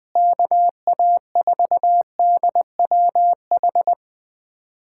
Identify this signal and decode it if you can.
Morse code, heard as a keyed tone, about 21 wpm, 700 Hz. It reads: KA4DWH
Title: USA Callsign DE Callsign - 6 Characters - 20wpm